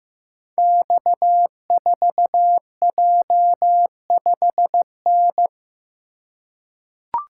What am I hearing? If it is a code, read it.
Morse code, heard as a keyed tone, about 15 wpm, 700 Hz. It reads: X4J5N
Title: Code Groups - 5 Letter:Numbers - 15wpm